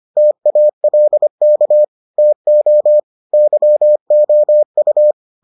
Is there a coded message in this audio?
Yes: Morse code from 0.2 to 5.1 s, heard as a keyed tone, about 25 wpm, 600 Hz. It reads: TALK TO YOU